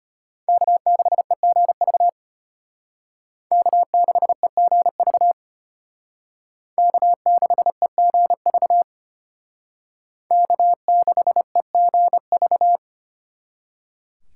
Morse code, heard as a keyed tone, about 33 wpm, 700 Hz. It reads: K6EGV K6EGV K6EGV K6EGV